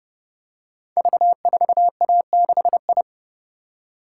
Morse code, heard as a keyed tone, about 30 wpm, 700 Hz. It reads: V4A6I
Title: Code Groups - 5 Letter:Numbers - 30wpm